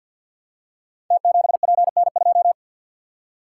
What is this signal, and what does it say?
Morse code, heard as a keyed tone, about 47 wpm, 700 Hz. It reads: T7PN2
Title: Code Groups - 5 Letter:Numbers - 50wpm